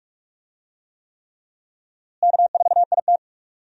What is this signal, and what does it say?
Morse code, heard as a keyed tone, about 46 wpm, 700 Hz. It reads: K4IT
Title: USA Call Signs - 4 Characters - Repeated 3x - 45wpm